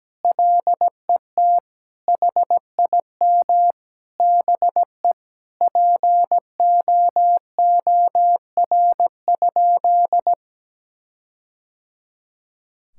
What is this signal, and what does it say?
Morse code, heard as a keyed tone, about 17 wpm, 700 Hz. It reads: LET HIM BE POOR?
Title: Sentences from Top 1600 Words 17wpm